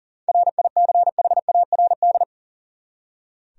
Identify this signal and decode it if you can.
Morse code, heard as a keyed tone, about 37 wpm, 700 Hz. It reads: RICHARD
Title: USA Common Names - 40wpm